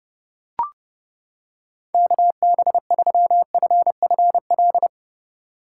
Morse code, heard as a keyed tone, about 29 wpm, 700 Hz. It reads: KB3FFL